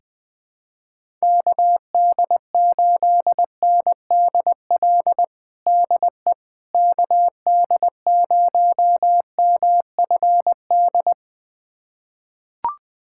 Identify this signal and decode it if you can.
Morse code, heard as a keyed tone, about 20 wpm, 700 Hz. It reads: KD8NDL DE KD0MFD
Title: USA Callsign DE Callsign 20wpm